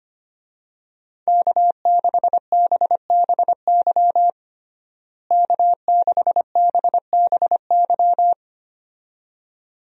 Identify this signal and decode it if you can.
Morse code, heard as a keyed tone, about 25 wpm, 700 Hz. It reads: K6BBY K6BBY
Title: USA Call Signs - 5 Characters - Repeated 2x - 25wpm